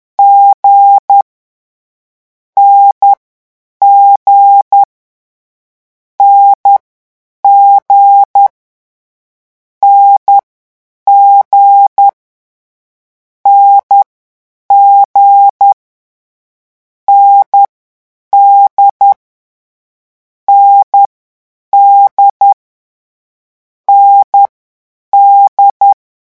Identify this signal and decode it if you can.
Morse code, heard as a keyed tone, about 10 wpm, 800 Hz. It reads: G N G N G N G N G N D N D N D
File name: morse mp3 / lecon4-vitesse 10